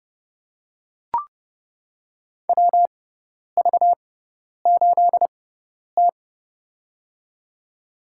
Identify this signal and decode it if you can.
Morse code, heard as a keyed tone, about 30 wpm, spaced as for 10 wpm, 700 Hz. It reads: WV8T
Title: USA Call Signs - 4 Characters - 6x Character Spacing - 30wpm